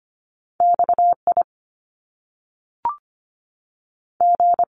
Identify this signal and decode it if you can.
Morse code, heard as a keyed tone, about 25 wpm, spaced as for 3 wpm, 700 Hz. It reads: X IG
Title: Top Two Letter Combinations in Words 25wpm